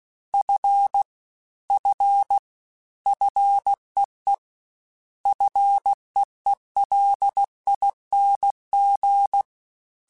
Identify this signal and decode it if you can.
Morse code, heard as a keyed tone, about 16 wpm, 800 Hz. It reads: F F FEE FEELING
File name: Single F mp3s / Single F Prefixes 3x2 Twice @15